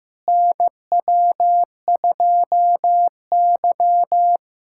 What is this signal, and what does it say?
Morse code, heard as a keyed tone, about 15 wpm, 700 Hz. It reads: NW2Y
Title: USA Callsign DE Callsign - 4 Characters - 15wpm